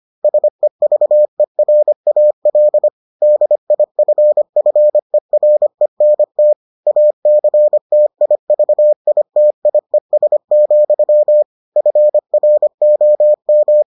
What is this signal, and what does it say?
Morse code, heard as a keyed tone, about 25 wpm, 600 Hz. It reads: SEVERAL DIFFERENT ACTIVITIES, FROM